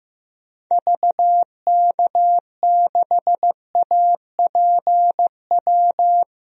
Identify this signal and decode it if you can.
Morse code, heard as a keyed tone, about 15 wpm, 700 Hz. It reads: VK6APW
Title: DXCC-100 Call Signs - 6 Characters - Repeated 3x - 15wpm